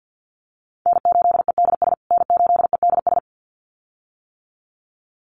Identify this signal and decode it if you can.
Morse code, heard as a keyed tone, about 50 wpm, 700 Hz. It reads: N8EDS N8EDS
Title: USA Call Signs Repeated 2x 50wpm